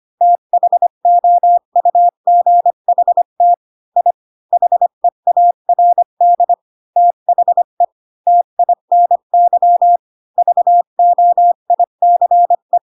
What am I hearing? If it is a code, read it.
Morse code, heard as a keyed tone, about 25 wpm, 700 Hz. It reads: THOUGHT I HEARD THE TINY VOICE
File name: Pinocchio / Pinocchio ch1 20 25 ik1hge